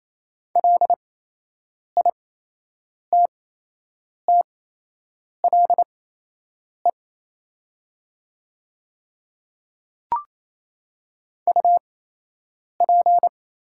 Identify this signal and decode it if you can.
Morse code, heard as a keyed tone, about 27 wpm, spaced as for 8 wpm, 700 Hz. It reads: LITTLE UP
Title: 8x Character Spacing - Top 200 Words - 28wpm